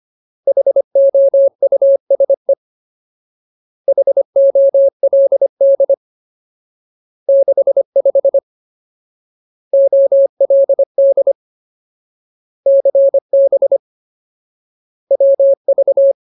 Morse code, heard as a keyed tone, about 25 wpm, 550 Hz. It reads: HOUSE HOLD 65 OLD CB WV
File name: Ultimate Word List / 25WPM Ultimate